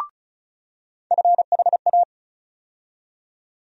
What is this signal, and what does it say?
Morse code, heard as a keyed tone, about 36 wpm, 700 Hz. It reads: FHA
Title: USA Call Sign Postfixes - 3 Letters 35wpm